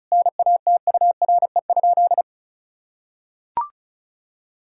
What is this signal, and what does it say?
Morse code, heard as a keyed tone, about 36 wpm, 700 Hz. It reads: NATURE?
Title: Sentences from Top 1000 Words - Review Part 5 - 35wpm